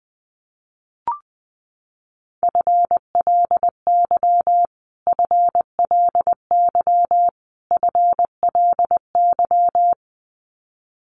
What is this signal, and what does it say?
Morse code, heard as a keyed tone, about 20 wpm, 700 Hz. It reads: FLY FLY FLY
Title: Top 500 Words Repeated 20wpm